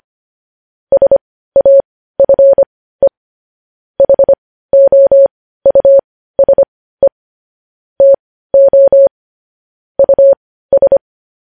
Morse code, heard as a keyed tone, about 25 wpm, spaced as for 15 wpm, 550 Hz. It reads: SAFE HOUSE TO US
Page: news headlines in morse code at 15 wpm